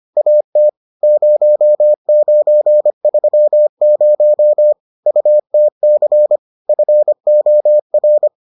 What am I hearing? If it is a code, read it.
Morse code, heard as a keyed tone, about 25 wpm, 600 Hz. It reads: AT 0930 UTC FOR